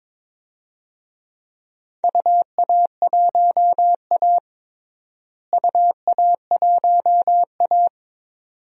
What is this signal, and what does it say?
Morse code, heard as a keyed tone, about 22 wpm, 700 Hz. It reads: UA1A UA1A
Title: DXCC-100 Call Signs - 4 Characters - Repeated 2x - 22wpm